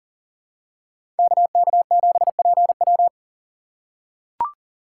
Morse code, heard as a keyed tone, about 40 wpm, 700 Hz. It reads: KK7PW
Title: USA Call Signs - 5 Characters - 4x Character Spacing - 40wpm